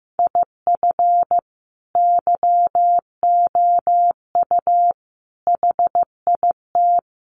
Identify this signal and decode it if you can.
Morse code, heard as a keyed tone, about 15 wpm, 700 Hz. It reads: IF YOU HIT
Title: A Sound Of Thunder 15wpm